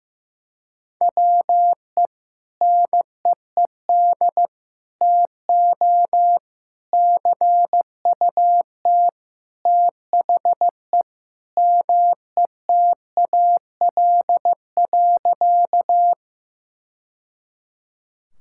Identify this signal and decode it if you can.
Morse code, heard as a keyed tone, about 15 wpm, 700 Hz. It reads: WE NEED TO CUT THE METAL.
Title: Sentences from Top 1000 Words - Review Part 6 - 15wpm